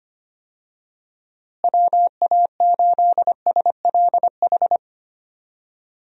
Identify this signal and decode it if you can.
Morse code, heard as a keyed tone, about 25 wpm, 700 Hz. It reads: WA8SLH
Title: USA Call Signs - 6 Characters - Repeated 3x - 25wpm